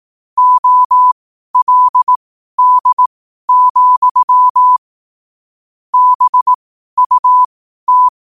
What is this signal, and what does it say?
Morse code, heard as a keyed tone, about 18 wpm, spaced as for 12 wpm, 1000 Hz. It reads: OLD, BUT
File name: qotdfiles / 12wpm